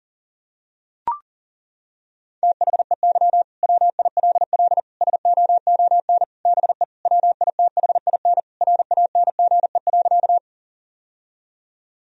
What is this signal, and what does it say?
Morse code, heard as a keyed tone, about 37 wpm, 700 Hz. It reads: THEY WILL SOON BE WITHIN RANGE.
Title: Sentences from Top 1000 Words - Review Part 4 - 40wpm